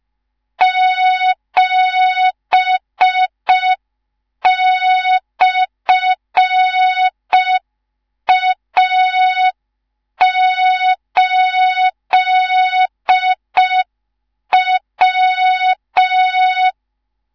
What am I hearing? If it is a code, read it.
Morse code, heard as a keyed tone, about 5 wpm, 3700 Hz. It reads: 7/A8W
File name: Morze-HA2MN / hang / 19 GYAKORLAT